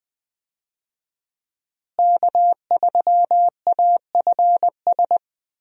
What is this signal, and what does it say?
Morse code, heard as a keyed tone, about 20 wpm, 700 Hz. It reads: K3AFS
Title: USA Call Signs - 5 Characters - Repeated 2x - 20wpm